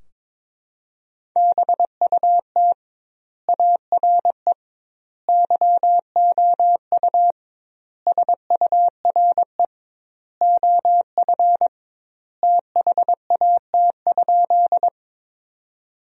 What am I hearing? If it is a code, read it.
Morse code, heard as a keyed tone, about 22 wpm, 700 Hz. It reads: BUT ARE YOU SURE OF THAT?
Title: 2x Word-Spacing - Sentences from Top 300 Words - 22wpm